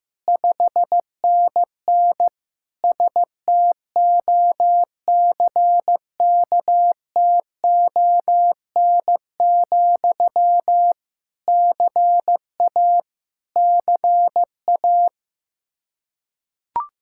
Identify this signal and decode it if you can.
Morse code, heard as a keyed tone, about 15 wpm, 700 Hz. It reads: 5NN STOCKTON, CA CA
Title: Practice QSO - Exchange 1 - Beginning - 15wpm